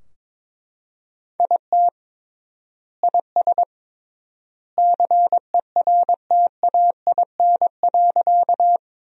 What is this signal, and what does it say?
Morse code, heard as a keyed tone, about 22 wpm, 700 Hz. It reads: IT IS CERTAIN.
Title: 3x Word-Spacing - Sentences from Top 500 Words - 22wpm